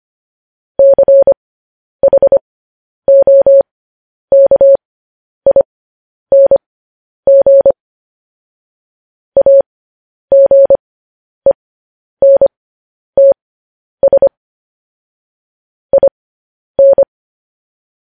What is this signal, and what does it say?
Morse code, heard as a keyed tone, about 25 wpm, spaced as for 10 wpm, 550 Hz. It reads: CHOKING AGENTS IN
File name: podcast / News headlines-1x-25x10